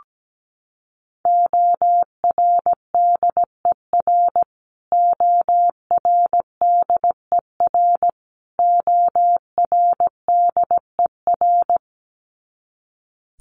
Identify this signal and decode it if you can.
Morse code, heard as a keyed tone, about 17 wpm, 700 Hz. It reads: ORDER ORDER ORDER
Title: Top 500 Words Repeated 17wpm